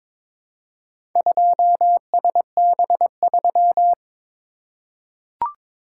Morse code, heard as a keyed tone, about 22 wpm, 700 Hz. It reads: 2SB3
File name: practice / 22wpm / Four Letter-Number Combinations - 22wpm